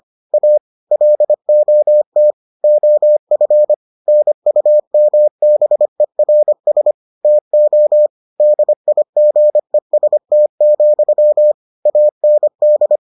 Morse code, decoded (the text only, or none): A LOT OF NUMBERS TO DIGEST, AND